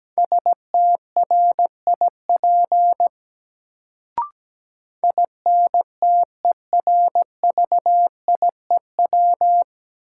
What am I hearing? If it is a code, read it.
Morse code, heard as a keyed tone, about 17 wpm, 700 Hz. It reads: STRIP INTERVIEW
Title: Sentences from Top 1600 Words - New Word Drill Alt - 17wpm